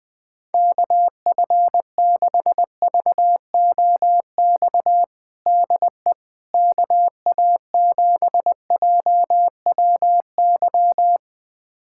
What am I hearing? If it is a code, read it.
Morse code, heard as a keyed tone, about 20 wpm, 700 Hz. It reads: KF6VOX DE KA7JWY
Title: USA Callsign DE Callsign - 6 Characters - 20wpm